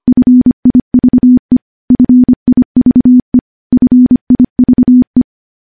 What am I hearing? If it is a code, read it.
Morse code, heard as a keyed tone, about 25 wpm, 250 Hz. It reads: FIVE FIVE FIVE
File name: morse / morse-training-25WPM